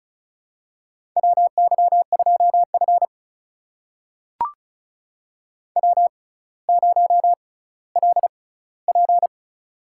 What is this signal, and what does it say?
Morse code, heard as a keyed tone, about 36 wpm, spaced as for 18 wpm, 700 Hz. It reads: WY2F W0LP
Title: USA Call Signs - 6x Character Spacing - 35wpm